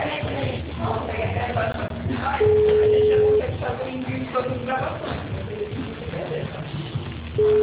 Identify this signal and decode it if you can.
Ringback tone (Europe, 425 Hz), with noise.